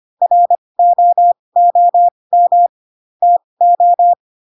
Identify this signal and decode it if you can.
Morse code, heard as a keyed tone, about 25 wpm, spaced as for 20 wpm, 700 Hz. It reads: ROOM TO
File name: Pinocchio / Pinocchio ch1 15 25 ik1hge